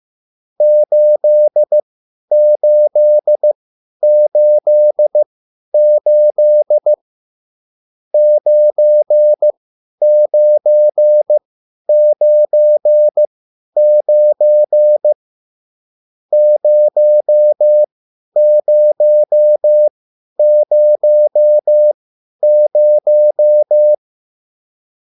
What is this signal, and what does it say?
Morse code, heard as a keyed tone, about 15 wpm, spaced as for 10 wpm, 600 Hz. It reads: 8888 9999 0000
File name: QSL / CW lesson